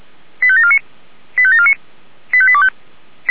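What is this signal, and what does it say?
Five-tone selective call (ZVEI), heard as quick runs of five tones, tones of about 70 ms. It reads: 85749 85749 85825
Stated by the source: ZVEI five-tone selective call received on SDR.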